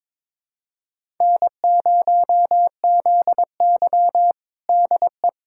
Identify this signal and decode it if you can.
Morse code, heard as a keyed tone, about 22 wpm, 700 Hz. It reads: N0ZY DE
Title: USA Callsign DE Callsign - 4 Characters - 22wpm